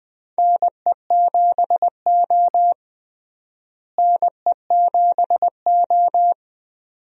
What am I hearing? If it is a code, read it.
Morse code, heard as a keyed tone, about 20 wpm, 700 Hz. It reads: NE7O NE7O
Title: USA Call Signs - 4 Characters - Repeated 2x - 20wpm